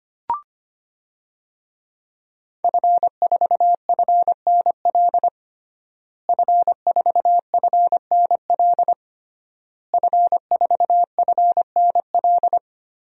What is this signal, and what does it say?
Morse code, heard as a keyed tone, about 25 wpm, 700 Hz. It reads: F4FNL F4FNL F4FNL
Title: DXCC-100 Call Signs - 5 Characters - Repeated 3x - 25wpm